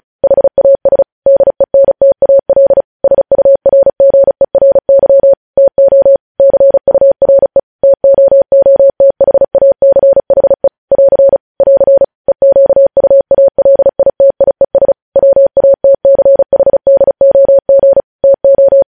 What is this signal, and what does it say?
Morse code, heard as a keyed tone, about 36 wpm, 550 Hz. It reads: HAS DENTAL SURGERY TO CURE TOOTHACHE <AR> <AR> EQUALITIES WATCHDOG TO
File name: podcast / News headlines-1x-35x35